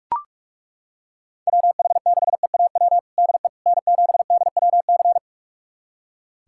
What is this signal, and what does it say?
Morse code, heard as a keyed tone, about 42 wpm, 700 Hz. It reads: WH6EAW DE N7DWC
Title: USA Callsign DE Callsign 45wpm